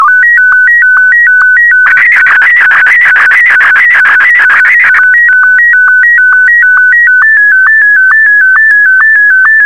SSTV (slow-scan television), Martin 1.
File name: sstv / M1-AX4MSL